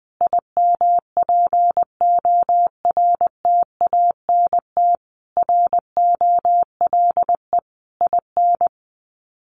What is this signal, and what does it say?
Morse code, heard as a keyed tone, about 20 wpm, 700 Hz. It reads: IMPORTANT ROLE IN
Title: Sets of 5 Words Encore 20wpm